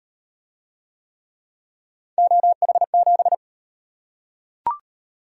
Morse code, heard as a keyed tone, about 35 wpm, 700 Hz. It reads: OH7